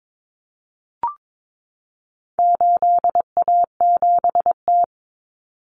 Morse code, heard as a keyed tone, about 22 wpm, 700 Hz. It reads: 8A7T
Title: Common Cut Numbers - 4-Digit 22wpm